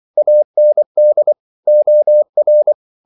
Morse code, heard as a keyed tone, about 24 wpm, 600 Hz. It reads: AND OR
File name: CW / Practice / 500MostCommonEnglishWords24